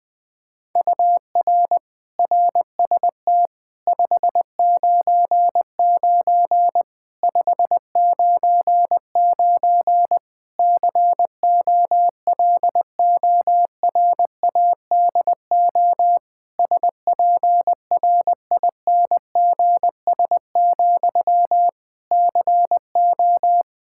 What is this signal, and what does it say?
Morse code, heard as a keyed tone, about 20 wpm, 700 Hz. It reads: UR RST 599 599 COLORADO SPRINGS, CO
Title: K1USN SST QSO Practice 20wpm